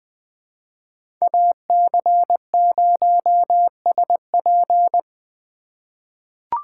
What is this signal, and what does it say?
Morse code, heard as a keyed tone, about 21 wpm, 700 Hz. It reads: AC0SP
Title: USA Call Signs - 4x Character Spacing - 20wpm